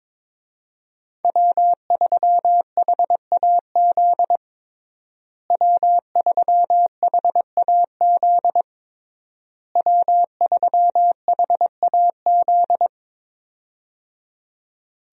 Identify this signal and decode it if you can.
Morse code, heard as a keyed tone, about 22 wpm, 700 Hz. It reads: W3HAZ W3HAZ W3HAZ
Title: USA Call Signs - 5 Characters - Repeated 3x - 22wpm